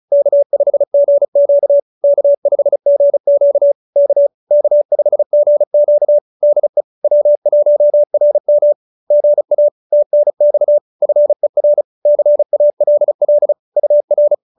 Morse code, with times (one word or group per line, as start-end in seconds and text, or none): 0.1-1.8 K5GQ
2.0-3.7 K5GQ
4.0-4.3 K
4.5-6.2 K5GQ
6.4-6.8 DE
7.0-8.7 W1RM
9.1-9.7 GA
9.9-10.8 TNX
11.0-11.8 FER
12.1-13.5 CALL
13.8-14.3 UR